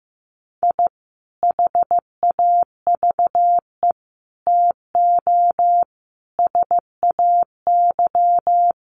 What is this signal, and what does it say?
Morse code, heard as a keyed tone, about 15 wpm, 700 Hz. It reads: I HAVE TO SAY
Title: Sets of 5 Words Encore 15wpm